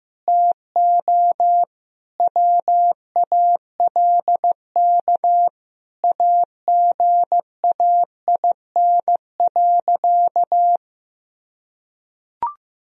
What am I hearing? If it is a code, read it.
Morse code, heard as a keyed tone, about 15 wpm, 700 Hz. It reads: TO WALK AGAIN.
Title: Sentences from Top 1000 Words - Review Part 1 - 15wpm